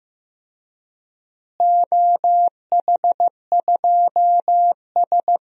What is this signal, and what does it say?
Morse code, heard as a keyed tone, about 16 wpm, 700 Hz. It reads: OH2S
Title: DXCC-100 Call Signs - 4 Characters - Repeated 2x - 15wpm